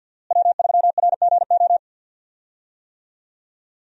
Morse code, heard as a keyed tone, about 49 wpm, 700 Hz. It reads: W3RGO
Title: USA Call Signs - 5 Characters - Repeated 3x - 50wpm